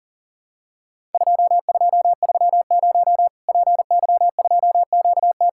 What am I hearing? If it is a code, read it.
Morse code, heard as a keyed tone, about 40 wpm, 700 Hz. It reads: 2230 PY2QT
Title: Worked All Europe DX Contest - QTC Exchange - 40wpm